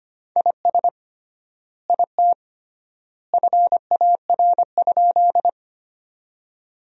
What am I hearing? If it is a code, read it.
Morse code, heard as a keyed tone, about 25 wpm, 700 Hz. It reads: IS IT FAR?
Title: 3x Word-Spacing - Sentences from Top 300 Words - 25wpm